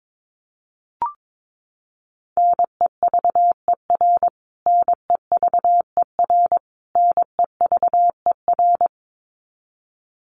Morse code, heard as a keyed tone, about 22 wpm, 700 Hz. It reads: NEVER NEVER NEVER
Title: Top 500 Words Repeated 22wpm